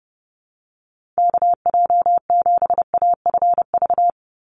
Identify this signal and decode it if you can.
Morse code, heard as a keyed tone, about 30 wpm, 700 Hz. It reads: KJ7AFV
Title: USA Call Signs 30wpm